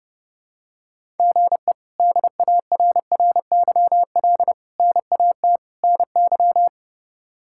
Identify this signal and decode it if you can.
Morse code, heard as a keyed tone, about 30 wpm, 700 Hz. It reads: GE DARRYL NAT NY
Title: K1USN SST QSO Practice - 30wpm